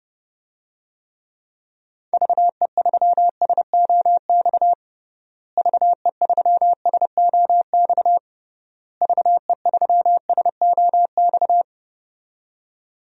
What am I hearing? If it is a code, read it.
Morse code, heard as a keyed tone, about 30 wpm, 700 Hz. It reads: VE3SOX VE3SOX VE3SOX
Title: DXCC-100 Call Signs - 6 Characters - Repeated 3x - 30wpm